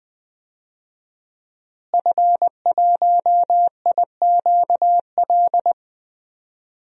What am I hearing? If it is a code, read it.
Morse code, heard as a keyed tone, about 21 wpm, 700 Hz. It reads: F1IQL
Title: DXCC-100 Call Signs - 5 Characters - Repeated 2x - 20wpm